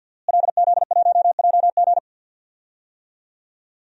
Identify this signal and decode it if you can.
Morse code, heard as a keyed tone, about 52 wpm, 700 Hz. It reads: LZ1JG
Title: DXCC-100 Call Signs - 5 Characters - Repeated 2x - 50wpm